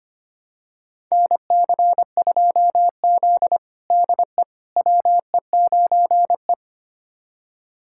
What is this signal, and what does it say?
Morse code, heard as a keyed tone, about 25 wpm, 700 Hz. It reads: NC2Z DE WE9E
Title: USA Callsign DE Callsign - 4 Characters - 25wpm